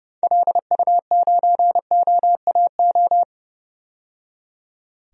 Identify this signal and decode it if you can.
Morse code, heard as a keyed tone, about 32 wpm, 700 Hz. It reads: LU9OAO
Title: DXCC-100 Call Signs - 6 Characters - 30wpm